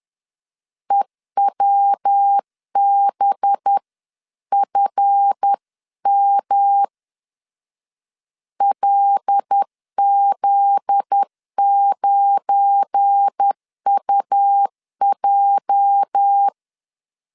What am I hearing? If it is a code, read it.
Morse code, heard as a keyed tone, about 10 wpm, 800 Hz. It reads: EWBFM LZ9UJ
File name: LEZIONI / 8 YJ45 / 8 YJ 10wpm 2651